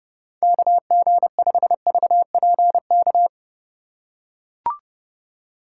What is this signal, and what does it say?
Morse code, heard as a keyed tone, about 31 wpm, 700 Hz. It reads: KG5VPK
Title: USA Call Signs - 2x Character Spacing - 30wpm